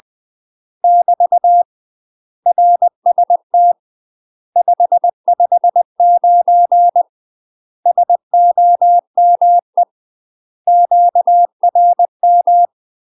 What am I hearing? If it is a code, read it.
Morse code, heard as a keyed tone, about 20 wpm, 700 Hz. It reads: <BT> RST 559 SOME QRM
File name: UK MASTER QSOs TRAINING / 20WPM / UK MASTER QSOs 0015